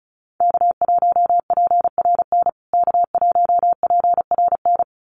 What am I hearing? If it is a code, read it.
Morse code, heard as a keyed tone, about 34 wpm, 700 Hz. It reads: K1PRN K1PRN
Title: USA Call Signs Repeated 2x 35wpm